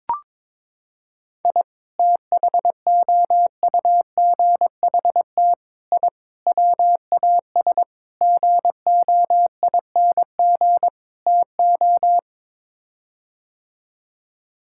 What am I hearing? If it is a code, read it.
Morse code, heard as a keyed tone, about 22 wpm, 700 Hz. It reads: I THOUGHT I WAS GOING TO
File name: practice / 22wpm / Sets of 6 Words - 22wpm